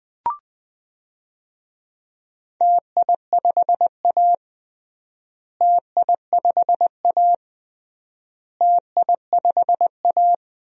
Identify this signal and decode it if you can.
Morse code, heard as a keyed tone, about 20 wpm, 700 Hz. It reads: TI5A TI5A TI5A
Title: DXCC-100 Call Signs - 4 Characters - Repeated 3x - 20wpm